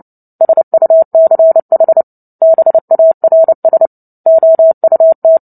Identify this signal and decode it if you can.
Morse code, heard as a keyed tone, about 30 wpm, 650 Hz. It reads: SUCH BARS OUT
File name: Book Two / 03-Chapter 3 - 29 WPM